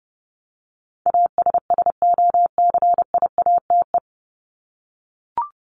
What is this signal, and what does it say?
Morse code, heard as a keyed tone, about 30 wpm, 700 Hz. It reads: ASSOCIATE
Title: ED Suffix - Top 100 Words with Base Words 30wpm